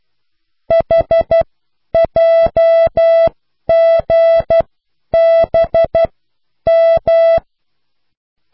Morse code, heard as a keyed tone, about 12 wpm, spaced as for 9 wpm, 660 Hz. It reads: HJGBM